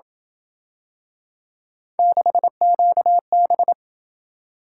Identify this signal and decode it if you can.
Morse code, heard as a keyed tone, about 28 wpm, 700 Hz. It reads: 6QB